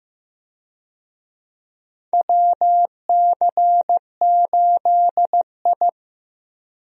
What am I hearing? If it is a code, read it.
Morse code, heard as a keyed tone, about 15 wpm, 700 Hz. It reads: WC8I
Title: USA Call Signs - 4 Characters - Repeated 2x - 15wpm